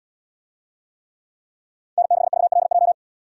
Morse code, heard as a keyed tone, about 77 wpm, 700 Hz. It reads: N6VLJ